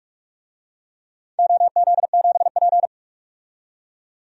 Morse code, heard as a keyed tone, about 46 wpm, 700 Hz. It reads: OZ7P
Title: DXCC-100 Call Signs - 4 Characters - 45wpm